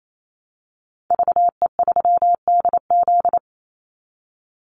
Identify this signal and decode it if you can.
Morse code, heard as a keyed tone, about 28 wpm, 700 Hz. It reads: VE3DZ
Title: Call Signs Repeated 3x - 28wpm